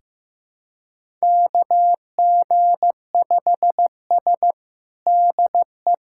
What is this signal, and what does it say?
Morse code, heard as a keyed tone, about 15 wpm, 700 Hz. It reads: KG5S DE
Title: USA Callsign DE Callsign - 4 Characters - 15wpm